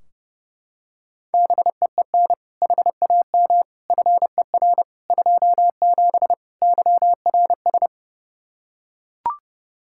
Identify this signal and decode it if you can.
Morse code, heard as a keyed tone, about 30 wpm, 700 Hz. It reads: BEEN HAM FER 27 YRS
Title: K1USN SST QSO Practice 30wpm